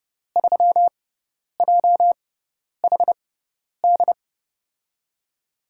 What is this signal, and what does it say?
Morse code, heard as a keyed tone, about 30 wpm, spaced as for 10 wpm, 700 Hz. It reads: 3JHD
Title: USA Call Signs - 6 Characters - 6x Character Spacing - 30wpm